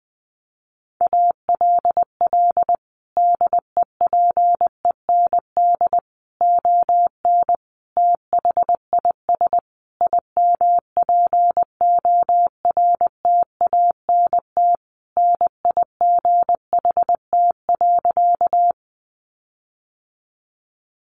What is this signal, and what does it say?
Morse code, heard as a keyed tone, about 20 wpm, 700 Hz. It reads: ALL DEPEND ON THIS IMPORTANT NIGHT.
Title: Sentences from Top 1300 Words 20wpm